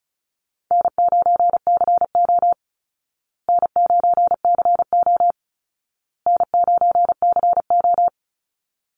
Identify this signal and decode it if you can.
Morse code, heard as a keyed tone, about 34 wpm, 700 Hz. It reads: N9CO N9CO N9CO
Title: Call Signs Repeated 2x - 35wpm